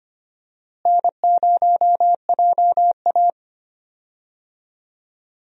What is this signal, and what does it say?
Morse code, heard as a keyed tone, about 25 wpm, 700 Hz. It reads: N0JA
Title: USA Call Signs - 4 Characters - Repeated 2x - 25wpm